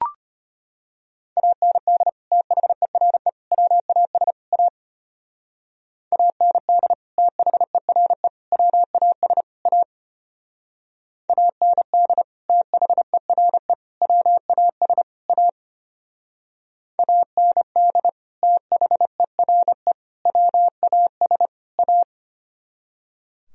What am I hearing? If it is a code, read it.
Morse code, heard as a keyed tone, about 32 wpm, 700 Hz. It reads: AND THERE WAS A AND THERE WAS A AND THERE WAS A AND THERE WAS A